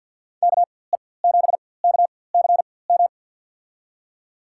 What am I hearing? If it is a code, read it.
Morse code, heard as a keyed tone, about 47 wpm, spaced as for 23 wpm, 700 Hz. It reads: KE7KCM
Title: USA Call Signs - 6 Characters - 4x Character Spacing - 50wpm